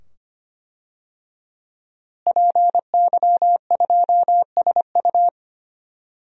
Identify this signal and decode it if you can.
Morse code, heard as a keyed tone, about 25 wpm, 700 Hz. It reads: PY2SU
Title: DXCC-100 Call Signs - 5 Characters - Repeated 2x - 25wpm